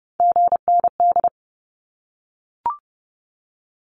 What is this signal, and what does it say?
Morse code, heard as a keyed tone, about 30 wpm, 700 Hz. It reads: GND
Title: Abbreviations Repeated 3x - 30wpm